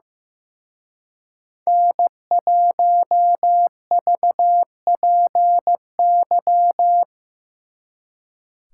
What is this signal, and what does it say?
Morse code, heard as a keyed tone, about 17 wpm, 700 Hz. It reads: N1VPY
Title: Speed Racing - USA 1x3 Call Signs 15wpm